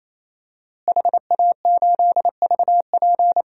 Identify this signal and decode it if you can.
Morse code, heard as a keyed tone, about 28 wpm, 700 Hz. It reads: HA8VP
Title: DXCC-100 Call Signs - 5 Characters - Repeated 3x - 28wpm